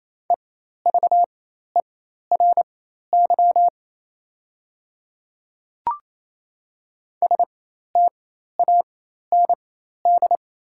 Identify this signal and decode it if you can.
Morse code, heard as a keyed tone, about 28 wpm, spaced as for 13 wpm, 690 Hz. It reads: EVERY STAND
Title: 4x Character Spacing - Top 300 Words - 28wpm